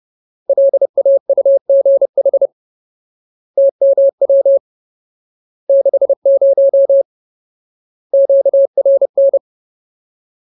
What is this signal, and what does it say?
Morse code, heard as a keyed tone, about 30 wpm, 550 Hz. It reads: LAUGH TMW 60 QRN
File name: Ultimate Word List / 30WPM Ultimate